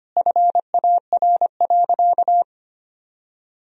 Morse code, heard as a keyed tone, about 25 wpm, 700 Hz. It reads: FAR.
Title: Sentences from Top 400 Words and Abbreviations 25wpm